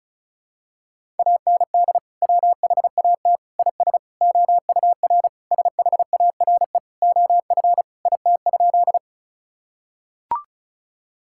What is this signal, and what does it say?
Morse code, heard as a keyed tone, about 34 wpm, 700 Hz. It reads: AND WHAT IS OUR SHARE OF IT?
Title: Sentences from Top 1000 Words - Review Part 6 - 35wpm